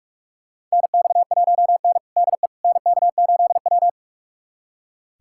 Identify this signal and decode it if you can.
Morse code, heard as a keyed tone, about 45 wpm, 700 Hz. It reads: NX1N DE NK8W
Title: USA Callsign DE Callsign - 4 Characters - 45wpm